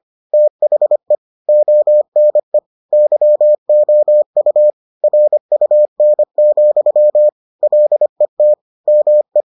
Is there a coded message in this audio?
Yes: Morse code from 0.3 to 9.4 s, heard as a keyed tone, about 25 wpm, 600 Hz. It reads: THE ONE YOU RUN, LET ME